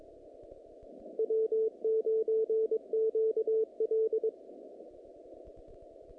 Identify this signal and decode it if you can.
Morse code, heard as a keyed tone, about 23 wpm, 440 Hz. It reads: W9QL
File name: contests / naqp-cw-k5tr-op-k5pi- -2100